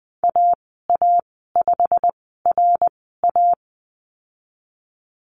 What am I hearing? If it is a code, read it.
Morse code, heard as a keyed tone, about 20 wpm, spaced as for 15 wpm, 700 Hz. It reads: AA5RA
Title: 167 Lesson 27g - Number 7 - Call Signs 20wpm